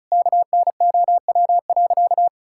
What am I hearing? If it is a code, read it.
Morse code, heard as a keyed tone, about 34 wpm, 700 Hz. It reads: KNOW.
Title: Sentences from Top 1600 Words 35wpm